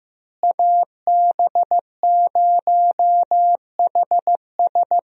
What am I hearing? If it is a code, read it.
Morse code, heard as a keyed tone, about 15 wpm, 700 Hz. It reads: AB0HS
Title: USA Call Signs - 5 Characters - Repeated 2x - 15wpm